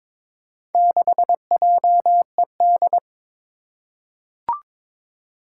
Morse code, heard as a keyed tone, about 22 wpm, 700 Hz. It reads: 6JED
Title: Four Challenging Letter-Number Combinations 22wpm